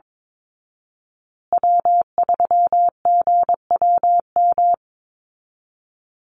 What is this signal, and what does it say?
Morse code, heard as a keyed tone, about 22 wpm, 700 Hz. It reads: W3GWM
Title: Call Signs Repeated 2x - 22wpm